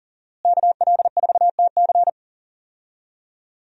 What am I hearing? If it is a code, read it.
Morse code, heard as a keyed tone, about 40 wpm, 700 Hz. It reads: KL4TC
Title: USA Call Signs - 5 Characters - Repeated 3x - 40wpm